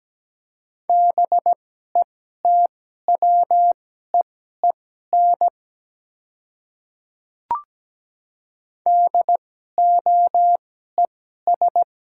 Morse code, heard as a keyed tone, about 17 wpm, spaced as for 12 wpm, 700 Hz. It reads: BETWEEN DOES
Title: 2x Character Spacing - Top 300 Words - 17wpm